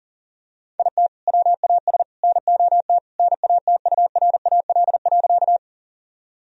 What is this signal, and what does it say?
Morse code, heard as a keyed tone, about 40 wpm, 700 Hz. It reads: IT WAS NOT NATURAL.
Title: Sentences from Top 1000 Words - Review Part 4 - 40wpm